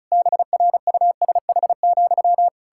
Morse code, heard as a keyed tone, about 36 wpm, 700 Hz. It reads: BRUSH,
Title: Pangram Challenge v2 - 35wpm